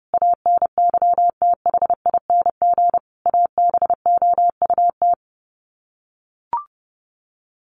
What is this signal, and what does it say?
Morse code, heard as a keyed tone, about 30 wpm, 700 Hz. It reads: ANYTHING ABOUT
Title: Sets of 5 Words 30wpm